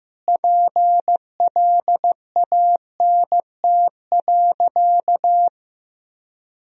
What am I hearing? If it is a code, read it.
Morse code, heard as a keyed tone, about 15 wpm, 700 Hz. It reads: PLANT.
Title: Sentences from Top 1000 Words - Review Part 1 - 15wpm